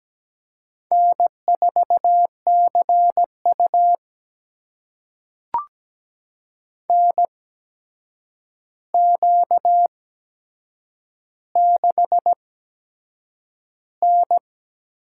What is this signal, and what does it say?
Morse code, heard as a keyed tone, about 17 wpm, spaced as for 7 wpm, 700 Hz. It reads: N4CU NQ6N
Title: USA Call Signs - 4 Characters - 8x Character Spacing - 17wpm